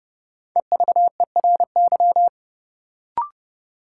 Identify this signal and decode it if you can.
Morse code, heard as a keyed tone, about 30 wpm, 700 Hz. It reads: EVERY
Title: Sets of 5 Words - New Word Drill - 30wpm